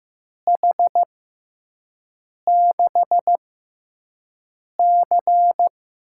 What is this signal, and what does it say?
Morse code, heard as a keyed tone, about 15 wpm, spaced as for 5 wpm, 700 Hz. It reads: H6C
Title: USA Call Signs - 6 Characters - 6x Character Spacing - 15wpm